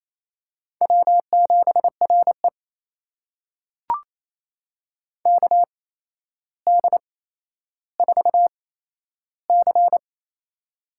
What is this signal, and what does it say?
Morse code, heard as a keyed tone, about 28 wpm, spaced as for 12 wpm, 700 Hz. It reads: W7RE KD4C
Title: USA Call Signs - 8x Character Spacing - 28wpm